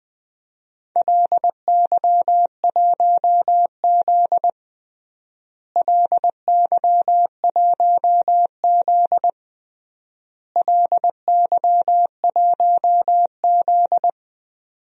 Morse code, heard as a keyed tone, about 20 wpm, 700 Hz. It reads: LY1Z LY1Z LY1Z
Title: DXCC-100 Call Signs - 4 Characters - Repeated 3x - 20wpm